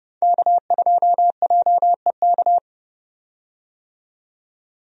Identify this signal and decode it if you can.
Morse code, heard as a keyed tone, about 31 wpm, 700 Hz. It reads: K2JEK
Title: USA Call Signs - 5 Characters - 30wpm